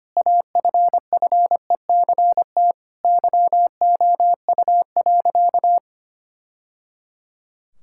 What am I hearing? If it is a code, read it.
Morse code, heard as a keyed tone, about 25 wpm, 700 Hz. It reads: AFFECT YOU.
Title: Sentences from Top 1000 Words - Review Part 6 - 25wpm